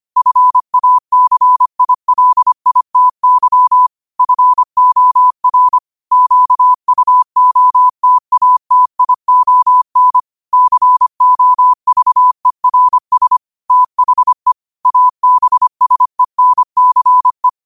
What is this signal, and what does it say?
Morse code, heard as a keyed tone, about 25 wpm, 1000 Hz. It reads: FACILITY FOR QUOTATION COVERS THE ABSENCE
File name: 25wpm